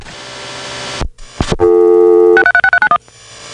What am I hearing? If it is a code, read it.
DTMF, heard as telephone touch tones, tones of about 70 ms. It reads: 636391